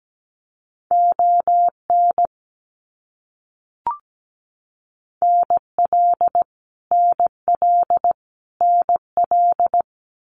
Morse code, heard as a keyed tone, about 17 wpm, 700 Hz. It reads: ON NL NL NL
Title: Canadian Provinces and Territories - Repeated 3x - 17wpm